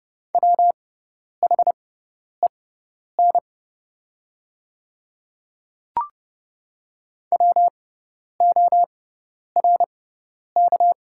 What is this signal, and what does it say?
Morse code, heard as a keyed tone, about 29 wpm, spaced as for 10 wpm, 690 Hz. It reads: WHEN WORK
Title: 6x Character Spacing - Top 300 Words - 30wpm